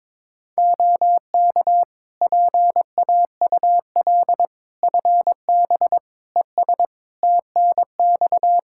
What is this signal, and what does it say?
Morse code, heard as a keyed tone, about 22 wpm, 700 Hz. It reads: OK PAUL FB ES TNX
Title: K1USN SST QSO Practice 22wpm